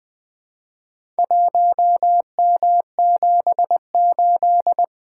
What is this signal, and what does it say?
Morse code, heard as a keyed tone, about 20 wpm, 700 Hz. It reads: 1M78
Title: Four Letter-Number Combinations - 20wpm